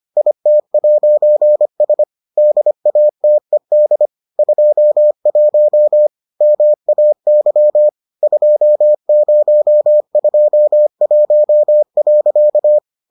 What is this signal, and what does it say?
Morse code, heard as a keyed tone, about 25 wpm, 600 Hz. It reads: IT'S DATED 21 MAY 2021.